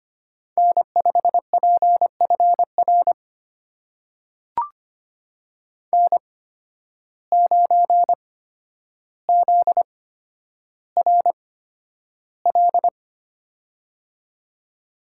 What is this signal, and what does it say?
Morse code, heard as a keyed tone, about 25 wpm, spaced as for 11 wpm, 700 Hz. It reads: N5PFR N9ZRL
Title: USA Call Signs - 5 Characters - 8x Character Spacing - 25wpm